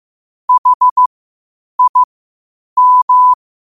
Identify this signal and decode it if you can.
Morse code, heard as a keyed tone, about 15 wpm, spaced as for 8 wpm, 1000 Hz. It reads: HIM
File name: qotdfiles / 8wpm